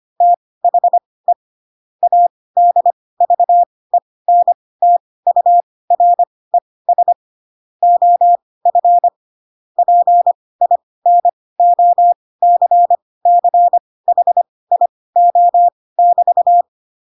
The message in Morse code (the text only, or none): THE ADVENTURES OF PINOCCHIO<BT>